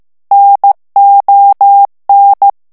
Morse code, heard as a keyed tone, about 15 wpm, 800 Hz. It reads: NON
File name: ITALIEN texte clair 1 vitesse 14 mots minute